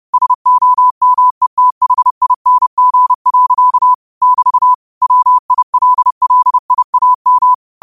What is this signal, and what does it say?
Morse code, heard as a keyed tone, about 30 wpm, 1000 Hz. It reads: SOMETHING. <BT> WILLIAM
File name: 30wpm